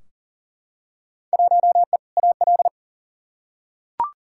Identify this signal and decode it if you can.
Morse code, heard as a keyed tone, about 40 wpm, 700 Hz. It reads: 1E AL
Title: Field Day Exchanges - 40wpm